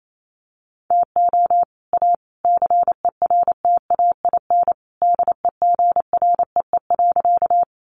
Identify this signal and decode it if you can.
Morse code, heard as a keyed tone, about 28 wpm, 700 Hz. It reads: TO A CERTAIN DEGREE.
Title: Sentences from Top 1500 Words 28wpm